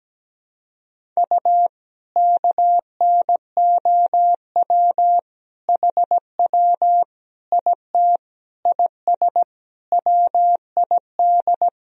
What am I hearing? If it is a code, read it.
Morse code, heard as a keyed tone, about 17 wpm, 700 Hz. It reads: U KNOW HW IT IS WID
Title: Sentences from Top 100 Words and Abbreviations 17wpm